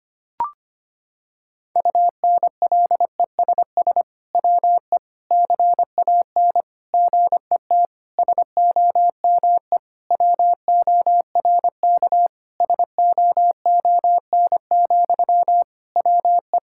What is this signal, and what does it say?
Morse code, heard as a keyed tone, about 25 wpm, 700 Hz. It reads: UNLESS WE CAN GET SOME WORK SOON, WE
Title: Sentences from Top 1000 Words - Review Part 6 - 25wpm